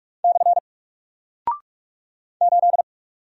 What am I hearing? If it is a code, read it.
Morse code, heard as a keyed tone, about 44 wpm, 700 Hz. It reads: / 8
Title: Single Character (Letter:Number:Punctuation:Prosign) 45wpm